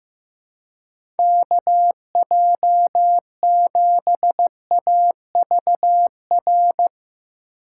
Morse code, heard as a keyed tone, about 15 wpm, 700 Hz. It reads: K J 7 A V R
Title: USA Call Signs - 6x Character Spacing - 15wpm